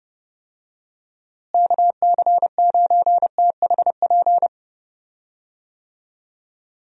Morse code, heard as a keyed tone, about 30 wpm, 700 Hz. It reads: KC9THP
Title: USA Call Signs - 6 Characters - Repeated 2x - 30wpm